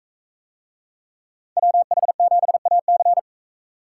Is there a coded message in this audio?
Yes: Morse code from 1.6 to 3.2 s, heard as a keyed tone, about 42 wpm, 700 Hz. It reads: WH7AC